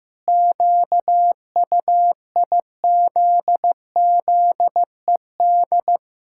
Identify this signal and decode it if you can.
Morse code, heard as a keyed tone, about 15 wpm, 700 Hz. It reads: QUIZZED
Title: Pangram Challenge v2 - 15wpm